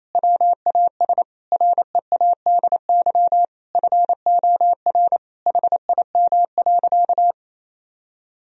Morse code, heard as a keyed tone, about 28 wpm, 700 Hz. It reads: WAS READY FOR HIM.
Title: Sentences from Top 1000 Words - Review Part 3 - 28wpm